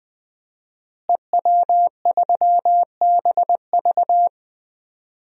Morse code, heard as a keyed tone, about 20 wpm, 700 Hz. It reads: EW3BV
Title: DXCC-100 Call Signs - 5 Characters - Repeated 2x - 20wpm